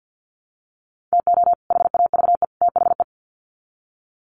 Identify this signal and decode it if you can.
Morse code, heard as a keyed tone, about 50 wpm, 700 Hz. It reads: TO HAVE THE
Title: Sets of 3 Words 50wpm